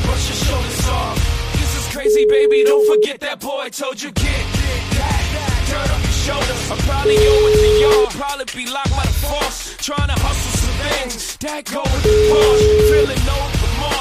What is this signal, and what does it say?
Ringback tone (Europe, 425 Hz), with speech.